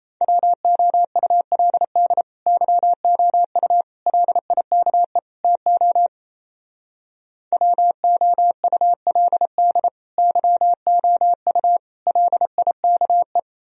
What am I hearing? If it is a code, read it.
Morse code, heard as a keyed tone, about 35 wpm, 700 Hz. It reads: WOULD YOU LIKE TO WOULD YOU LIKE